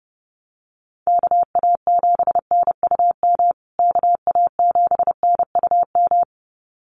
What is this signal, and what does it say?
Morse code, heard as a keyed tone, about 29 wpm, 700 Hz. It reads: KA7NUM KA7NUM
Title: USA Call Signs Repeated 2x 30wpm